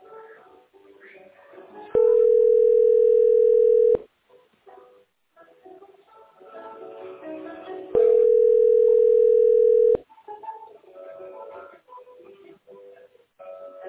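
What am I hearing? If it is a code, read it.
Ringback tone (North America), 440 + 480 Hz.
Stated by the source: Telephone call recorded on the tesspb.mdpbx PBX.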